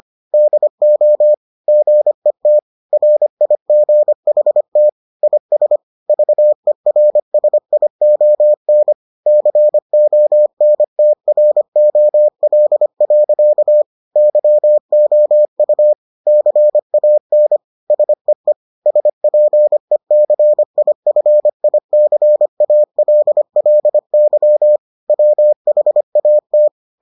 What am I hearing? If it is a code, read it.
Morse code, heard as a keyed tone, about 25 wpm, 600 Hz. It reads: DO GET RIGHT IS VERSION CONTROL. YOU CAN SEE SPECIFICALLY WHAT